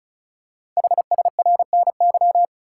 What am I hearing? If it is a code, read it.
Morse code, heard as a keyed tone, about 36 wpm, 690 Hz. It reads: HSRNY
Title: Code Groups - 5 Letters - 35wpm